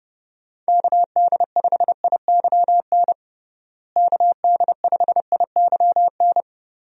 Morse code, heard as a keyed tone, about 30 wpm, 700 Hz. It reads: KD5IYN KD5IYN
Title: USA Call Signs - 6 Characters - Repeated 3x - 30wpm